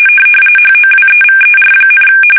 RTTY, about 45.45 baud. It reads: HOWEVER,
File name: fsk850 / battleshipnj